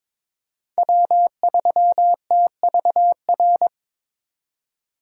Morse code, heard as a keyed tone, about 22 wpm, 700 Hz. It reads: W3TVR
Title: USA Call Signs - 2x Character Spacing - 22wpm